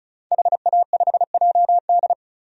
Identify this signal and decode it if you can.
Morse code, heard as a keyed tone, about 34 wpm, 700 Hz. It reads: HA5JD
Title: DXCC-100 Call Signs - 5 Characters - 35wpm